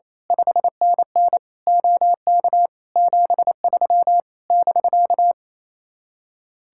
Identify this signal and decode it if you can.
Morse code, heard as a keyed tone, about 26 wpm, 700 Hz. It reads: 5NN OK 73 <BK>
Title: Speed-Racing Common QSO Phrases - 28wpm